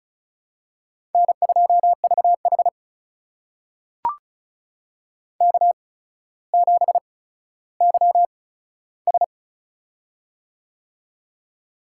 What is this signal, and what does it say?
Morse code, heard as a keyed tone, about 36 wpm, spaced as for 15 wpm, 700 Hz. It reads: N2VH K7YS
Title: USA Call Signs - 4 Characters - 8x Character Spacing - 35wpm